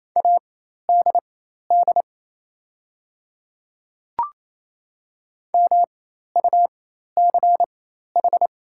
Morse code, heard as a keyed tone, about 27 wpm, spaced as for 13 wpm, 690 Hz. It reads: ADD MUCH
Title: 4x Character Spacing - Top 200 Words - 28wpm